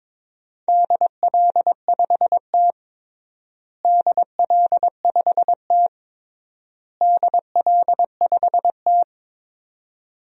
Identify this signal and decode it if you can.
Morse code, heard as a keyed tone, about 22 wpm, 700 Hz. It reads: DL5T DL5T DL5T
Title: DXCC-100 Call Signs - 4 Characters - Repeated 3x - 22wpm